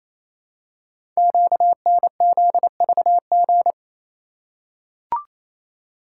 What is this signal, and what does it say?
Morse code, heard as a keyed tone, about 27 wpm, 700 Hz. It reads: QNZVG
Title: Code Groups - 5 Letters - 28wpm